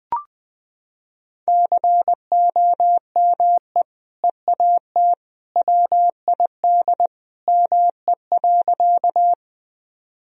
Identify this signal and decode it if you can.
Morse code, heard as a keyed tone, about 20 wpm, 700 Hz. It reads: COME EAT WID ME.
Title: Sentences from Top 300 Words and Abbreviations 20wpm